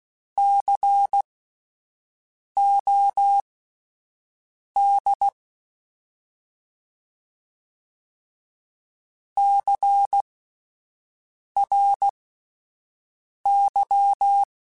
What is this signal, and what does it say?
Morse code, heard as a keyed tone, about 16 wpm, spaced as for 5 wpm, 800 Hz. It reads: COD CRY